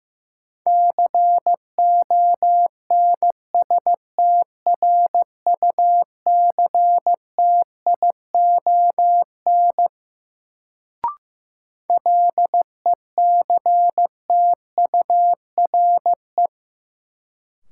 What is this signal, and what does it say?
Morse code, heard as a keyed tone, about 15 wpm, 700 Hz. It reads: CONSTRUCTION LECTURE
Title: Sentences from Top 1600 Words - New Word Drill Alt - 15wpm